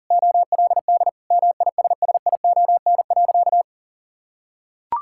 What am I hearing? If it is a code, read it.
Morse code, heard as a keyed tone, about 40 wpm, 700 Hz. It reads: OLD MISSION.
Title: Sentences from Top 1600 Words 40wpm